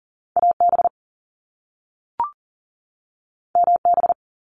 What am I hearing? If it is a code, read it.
Morse code, heard as a keyed tone, about 40 wpm, 700 Hz. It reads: AB MB
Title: Canadian Provinces and Territories 40wpm